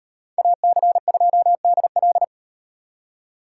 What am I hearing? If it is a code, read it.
Morse code, heard as a keyed tone, about 35 wpm, 700 Hz. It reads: AC2DL